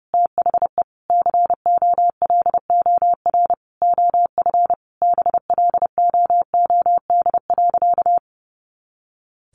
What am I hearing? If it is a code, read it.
Morse code, heard as a keyed tone, about 30 wpm, 700 Hz. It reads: THE COLOR OF BLOOD.
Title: A Sound Of Thunder 30wpm